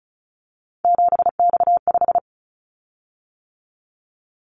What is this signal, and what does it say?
Morse code, heard as a keyed tone, about 36 wpm, 700 Hz. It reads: 7X5
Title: Callsign Prefixes for Top 100 DXCC - 35wpm